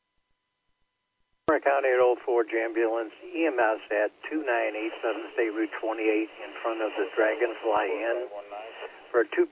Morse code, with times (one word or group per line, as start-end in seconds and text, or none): none